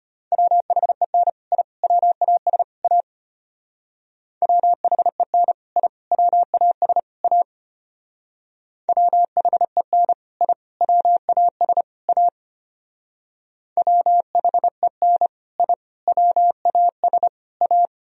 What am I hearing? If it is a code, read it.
Morse code, heard as a keyed tone, about 31 wpm, 700 Hz. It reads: WHEN I WAS A WHEN I WAS A WHEN I WAS A WHEN I WAS A
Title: Speed-Racing - Sets of 4 Words 25wpm